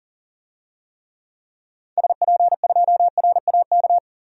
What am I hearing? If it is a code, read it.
Morse code, heard as a keyed tone, about 37 wpm, 700 Hz. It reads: SP2RAK
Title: DXCC-100 Call Signs - 6 Characters - 40wpm